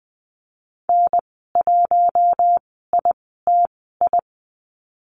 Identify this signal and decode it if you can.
Morse code, heard as a keyed tone, about 20 wpm, spaced as for 14 wpm, 700 Hz. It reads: N1ITI
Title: 042 Lesson 8g - Number 1 - Call Signs 20wpm